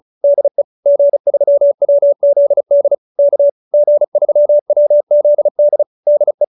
Morse code, heard as a keyed tone, about 34 wpm, 590 Hz. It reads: DE G3WZD K G3WZD DE